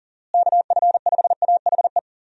Morse code, heard as a keyed tone, about 40 wpm, 700 Hz. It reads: KF5AHE
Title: USA Call Signs - 6 Characters - Repeated 2x - 40wpm